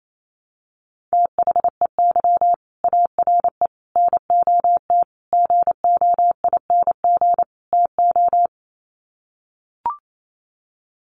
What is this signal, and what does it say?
Morse code, heard as a keyed tone, about 28 wpm, 700 Hz. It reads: THEY ARE NOT GOING TO
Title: Sets of 5 Words Encore2 28wpm